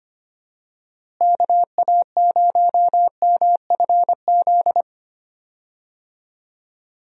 Morse code, heard as a keyed tone, about 25 wpm, 700 Hz. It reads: KA0MFZ
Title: USA Call Signs - 6 Characters - Repeated 3x - 25wpm